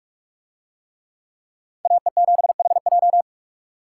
Morse code, heard as a keyed tone, about 44 wpm, 700 Hz. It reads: AE7HJ